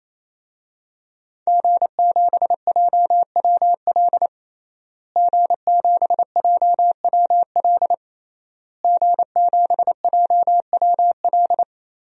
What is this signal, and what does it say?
Morse code, heard as a keyed tone, about 28 wpm, 700 Hz. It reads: G7JWL G7JWL G7JWL
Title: DXCC-100 Call Signs - 5 Characters - Repeated 3x - 28wpm